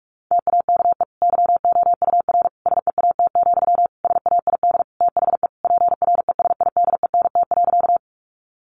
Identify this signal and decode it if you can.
Morse code, heard as a keyed tone, about 44 wpm, 700 Hz. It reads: TAKE YOUR SEAT, SAID THE PRESIDENT.
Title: Sentences from Top 1500 Words 45wpm